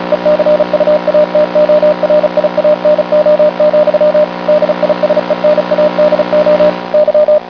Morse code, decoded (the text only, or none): ECUATORIANO, DISENADO Y